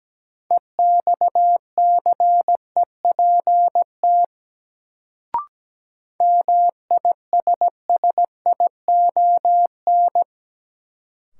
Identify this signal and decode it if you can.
Morse code, heard as a keyed tone, about 17 wpm, 700 Hz. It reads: EXCEPT MISSION
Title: Sentences from Top 1600 Words - New Word Drill - 17wpm